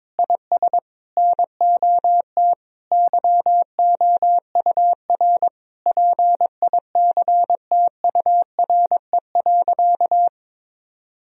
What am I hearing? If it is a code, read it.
Morse code, heard as a keyed tone, about 22 wpm, 700 Hz. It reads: IS NOT YOUR PICTURE.
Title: Sentences from Top 1000 Words - Review Part 1 - 22wpm